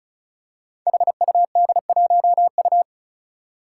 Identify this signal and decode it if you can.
Morse code, heard as a keyed tone, about 36 wpm, 700 Hz. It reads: HUD1U
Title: Five Challenging Letter-Number Combinations 35wpm